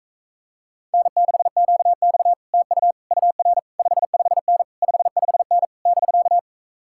Morse code, heard as a keyed tone, about 45 wpm, 700 Hz. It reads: N6QX TU UR 55N 55N <BK>